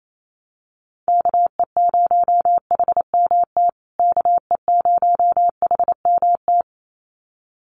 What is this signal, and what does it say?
Morse code, heard as a keyed tone, about 28 wpm, 700 Hz. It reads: KE0HMT KE0HMT
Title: USA Call Signs Repeated 2x 28wpm